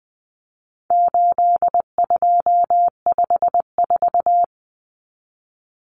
Morse code, heard as a keyed tone, about 20 wpm, 700 Hz. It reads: 8254
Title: Common Cut Numbers - 4-Digit 20wpm